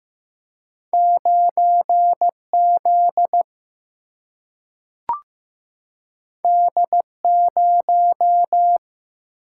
Morse code, heard as a keyed tone, about 15 wpm, 700 Hz. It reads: 9Z D0
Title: Single-Letter Single-Number 15wpm